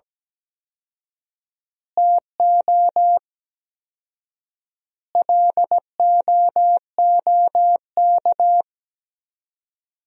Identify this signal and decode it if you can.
Morse code, heard as a keyed tone, about 17 wpm, 700 Hz. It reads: TO LOOK
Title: 4x Word-Spacing - Sentences from Top 200 Words - 17wpm